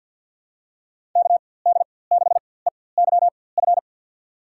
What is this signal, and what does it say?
Morse code, heard as a keyed tone, about 50 wpm, spaced as for 23 wpm, 700 Hz. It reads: KD6EYF
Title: USA Call Signs - 4x Character Spacing - 50wpm